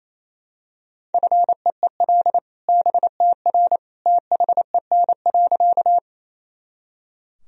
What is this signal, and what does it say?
Morse code, heard as a keyed tone, about 27 wpm, 700 Hz. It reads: FEEL BTR THEN.
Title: Sentences from Top 400 Words and Abbreviations 28wpm